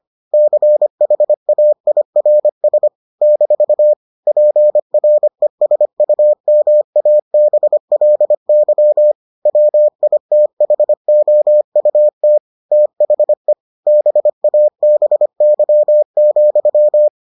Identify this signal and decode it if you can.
Morse code, heard as a keyed tone, about 25 wpm, 600 Hz. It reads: CHAIRS - PRESUMABLY WITHOUT THE BABY,